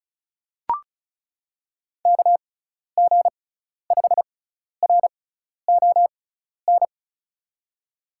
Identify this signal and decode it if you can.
Morse code, heard as a keyed tone, about 36 wpm, spaced as for 12 wpm, 700 Hz. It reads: KG5RON
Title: USA Call Signs - 6x Character Spacing - 35wpm